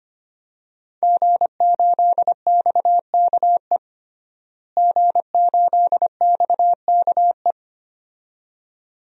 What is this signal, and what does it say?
Morse code, heard as a keyed tone, about 25 wpm, 700 Hz. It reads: G8XKE G8XKE
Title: DXCC-100 Call Signs - 5 Characters - Repeated 2x - 25wpm